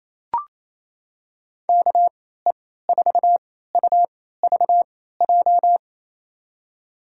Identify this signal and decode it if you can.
Morse code, heard as a keyed tone, about 28 wpm, spaced as for 16 wpm, 700 Hz. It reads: KE4UVJ
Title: USA Call Signs - 6 Characters - 3x Character Spacing - 28wpm